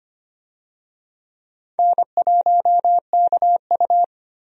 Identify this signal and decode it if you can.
Morse code, heard as a keyed tone, about 25 wpm, 700 Hz. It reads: N1KU
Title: USA Call Signs - 4 Characters - Repeated 2x - 25wpm